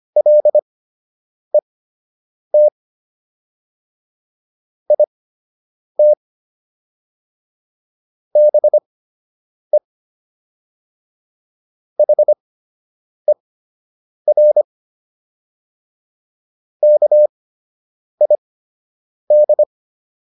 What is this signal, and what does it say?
Morse code, heard as a keyed tone, about 25 wpm, spaced as for 8 wpm, 600 Hz. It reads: LET IT BE HER KID